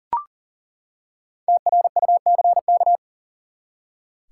Morse code, heard as a keyed tone, about 40 wpm, 700 Hz. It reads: TRUCK
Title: Five Letter Words - 40wpm